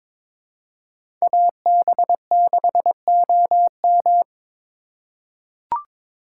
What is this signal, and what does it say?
Morse code, heard as a keyed tone, about 21 wpm, 700 Hz. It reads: AB6OM
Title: USA Call Signs - 6x Character Spacing - 22wpm